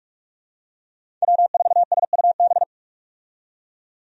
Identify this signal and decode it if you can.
Morse code, heard as a keyed tone, about 46 wpm, 700 Hz. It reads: W4SUB
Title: USA Call Signs - 5 Characters - 45wpm